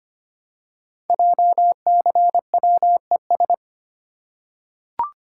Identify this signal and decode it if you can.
Morse code, heard as a keyed tone, about 25 wpm, 700 Hz. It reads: JCWES
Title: Code Groups - 5 Letters - 25wpm